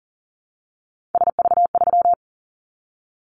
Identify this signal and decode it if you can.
Morse code, heard as a keyed tone, about 43 wpm, 700 Hz. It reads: SV3
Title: Callsign Prefixes for Top 50 DXCC - 40wpm